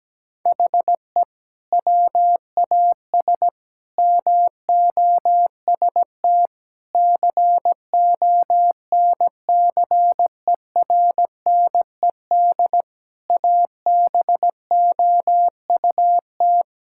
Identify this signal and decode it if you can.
Morse code, heard as a keyed tone, about 17 wpm, 700 Hz. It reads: HE WAS MOST CONCERNED ABOUT
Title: Sentences from Top 1000 Words - Review Part 6 - 17wpm